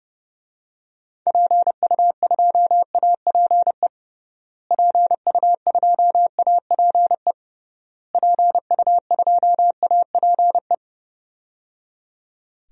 Morse code, heard as a keyed tone, about 30 wpm, 700 Hz. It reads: PU2APE PU2APE PU2APE
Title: DXCC-100 Call Signs - 6 Characters - Repeated 3x - 30wpm